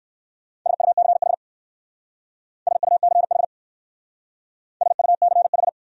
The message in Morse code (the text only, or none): SUCH SUCH SUCH